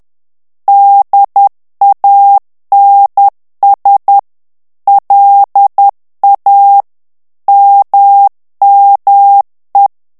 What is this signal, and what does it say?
Morse code, heard as a keyed tone, about 10 wpm, 800 Hz. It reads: DANS LA MME
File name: morse mp3 / FRANCAIS TEXTE CLAIR 2 VITESSE 10 MOTS MINUTES